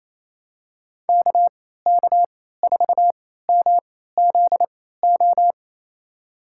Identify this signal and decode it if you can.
Morse code, heard as a keyed tone, about 28 wpm, spaced as for 16 wpm, 700 Hz. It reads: KK4MZO
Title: USA Call Signs - 6 Characters - 3x Character Spacing - 28wpm